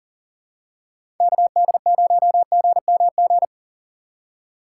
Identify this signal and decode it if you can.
Morse code, heard as a keyed tone, about 40 wpm, 700 Hz. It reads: KD0GMG
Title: USA Call Signs - 3x Character Spacing - 40wpm